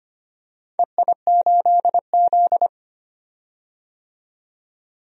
Morse code, heard as a keyed tone, about 25 wpm, 700 Hz. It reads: EI8Z
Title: DXCC-100 Call Signs - 4 Characters - 25wpm